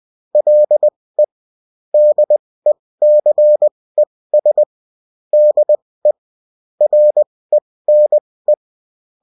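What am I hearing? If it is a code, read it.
Morse code, heard as a keyed tone, about 20 wpm, spaced as for 16 wpm, 600 Hz. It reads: LE DECES DE RENE
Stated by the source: Morse code bulletin of a French amateur CW union, sent as text and in CW.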